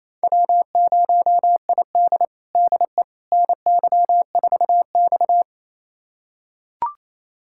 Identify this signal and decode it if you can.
Morse code, heard as a keyed tone, about 28 wpm, 700 Hz. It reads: W0ID DE NY4X
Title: USA Callsign DE Callsign - 4 Characters - 28wpm